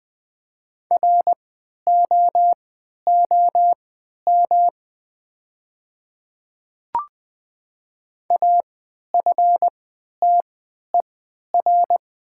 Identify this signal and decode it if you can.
Morse code, heard as a keyed tone, about 20 wpm, spaced as for 11 wpm, 700 Hz. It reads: ROOM AFTER
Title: 3x Character Spacing - Top 500 Words - 20wpm